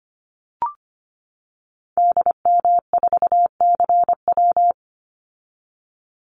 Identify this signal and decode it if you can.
Morse code, heard as a keyed tone, about 25 wpm, 700 Hz. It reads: DM4CW
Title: Call Signs - 25wpm